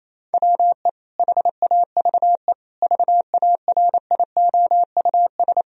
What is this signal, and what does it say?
Morse code, heard as a keyed tone, about 28 wpm, 700 Hz. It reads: WE HAVE VARIOUS
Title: Sentences from Top 1000 Words - Review Part 4 - 28wpm